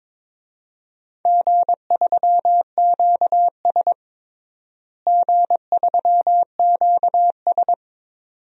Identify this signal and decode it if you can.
Morse code, heard as a keyed tone, about 21 wpm, 700 Hz. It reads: G3QS G3QS
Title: DXCC-100 Call Signs - 4 Characters - Repeated 2x - 22wpm